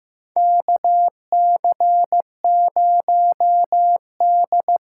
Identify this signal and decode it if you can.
Morse code, heard as a keyed tone, about 15 wpm, 700 Hz. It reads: KC0D
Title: USA Call Signs - 6 Characters - 8x Character Spacing - 15wpm